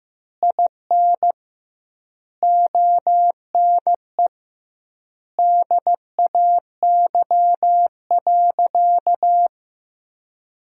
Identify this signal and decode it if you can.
Morse code, heard as a keyed tone, about 15 wpm, 700 Hz. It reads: IN ONE DAY.
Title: 2x Word-Spacing - Sentences from Top 500 Words - 15wpm